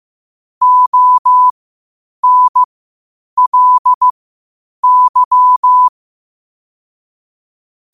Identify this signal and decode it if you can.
Morse code, heard as a keyed tone, about 15 wpm, spaced as for 8 wpm, 1000 Hz. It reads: ONLY
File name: qotdfiles / 8wpm